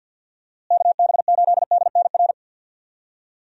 Morse code, heard as a keyed tone, about 52 wpm, 700 Hz. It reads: KB8DNR
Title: USA Call Signs - 6 Characters - Repeated 2x - 50wpm